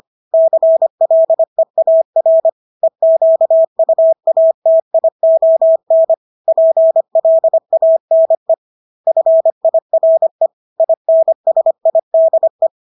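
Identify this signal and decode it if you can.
Morse code, heard as a keyed tone, about 25 wpm, 650 Hz. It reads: CLEAR EQUATION PLANE FIRE INSIDE
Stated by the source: Morse practice of English words at 25 wpm from CW med Gnister.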